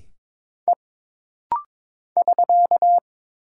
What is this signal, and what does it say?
Morse code, heard as a keyed tone, about 22 wpm, 700 Hz. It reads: E <SK>
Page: This rapid-fire timing is geared to push your ICR (Instant Character Recognition) skills.